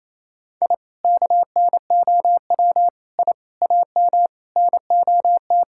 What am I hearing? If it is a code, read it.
Morse code, heard as a keyed tone, about 27 wpm, 700 Hz. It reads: I KNOW I AM NOT
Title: Sentences from Top 1000 Words - Review Part 5 - 28wpm